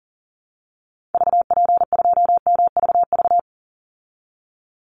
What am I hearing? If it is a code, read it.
Morse code, heard as a keyed tone, about 40 wpm, 700 Hz. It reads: VP2MVV
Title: Call Signs Repeated 3x - 40wpm